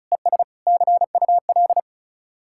Morse code, heard as a keyed tone, about 43 wpm, 700 Hz. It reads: ES CUL
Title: Speed-Racing Common QSO Phrases - 35wpm